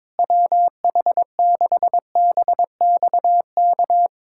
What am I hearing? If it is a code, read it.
Morse code, heard as a keyed tone, about 22 wpm, 700 Hz. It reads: WH6BXK
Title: USA Callsign DE Callsign - 6 Characters - 22wpm